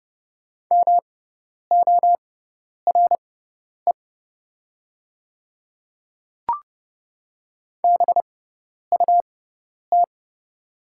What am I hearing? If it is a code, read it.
Morse code, heard as a keyed tone, about 30 wpm, spaced as for 10 wpm, 700 Hz. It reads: MORE BUT
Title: 6x Character Spacing - Top 200 Words - 30wpm